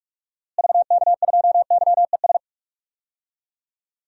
Morse code, heard as a keyed tone, about 46 wpm, 700 Hz. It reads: VK2YES
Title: DXCC-100 Call Signs - 6 Characters - Repeated 2x - 45wpm